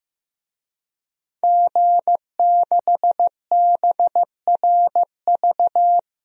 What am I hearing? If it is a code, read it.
Morse code, heard as a keyed tone, about 16 wpm, 700 Hz. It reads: G6BRV
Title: DXCC-100 Call Signs - 5 Characters - 15wpm